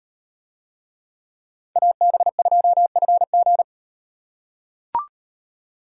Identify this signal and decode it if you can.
Morse code, heard as a keyed tone, about 38 wpm, 700 Hz. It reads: AB2FG